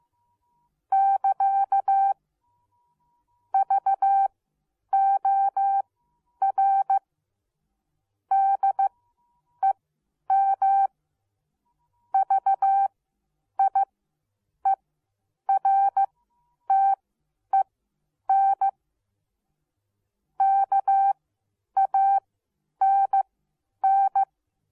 Morse code, heard as a keyed tone, about 15 wpm, spaced as for 9 wpm, 780 Hz. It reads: <KA> VOR DEM VIERTEN KANN